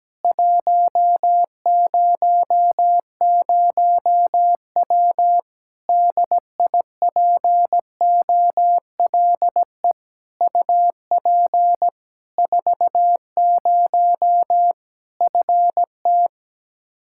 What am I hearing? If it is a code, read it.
Morse code, heard as a keyed tone, about 17 wpm, 700 Hz. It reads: 100W DIPOLE UP 40 FT
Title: Practice QSO - Exchange 2 - Info - 17wpm